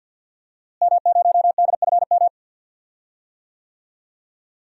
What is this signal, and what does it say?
Morse code, heard as a keyed tone, about 52 wpm, 700 Hz. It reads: M0DFM
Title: DXCC-100 Call Signs - 5 Characters - Repeated 3x - 50wpm